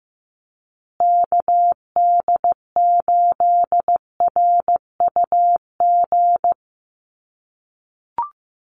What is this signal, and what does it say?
Morse code, heard as a keyed tone, about 15 wpm, 700 Hz. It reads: KD8RUG
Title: USA Call Signs Repeated 2x 15wpm